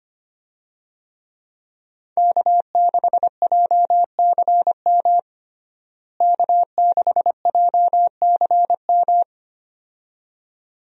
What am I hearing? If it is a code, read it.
Morse code, heard as a keyed tone, about 25 wpm, 700 Hz. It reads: K6JCM K6JCM
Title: USA Call Signs - 5 Characters - Repeated 2x - 25wpm